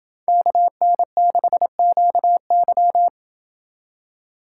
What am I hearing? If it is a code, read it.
Morse code, heard as a keyed tone, about 28 wpm, 700 Hz. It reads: KN6QY